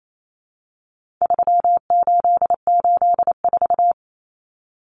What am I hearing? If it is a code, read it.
Morse code, heard as a keyed tone, about 28 wpm, 700 Hz. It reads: 3884
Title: Common Cut Numbers - 4-Digit 28wpm